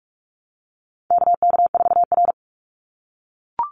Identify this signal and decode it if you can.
Morse code, heard as a keyed tone, about 44 wpm, 700 Hz. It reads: KK4R
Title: Call Signs - 45wpm